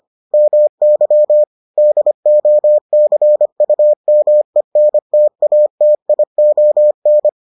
Morse code, heard as a keyed tone, about 25 wpm, 600 Hz. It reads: MY DOCUMENTATION